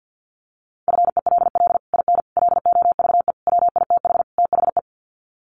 Morse code, heard as a keyed tone, about 50 wpm, 700 Hz. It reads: FELL IN LOVE WITH THE
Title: Sets of 5 Words Encore2 50wpm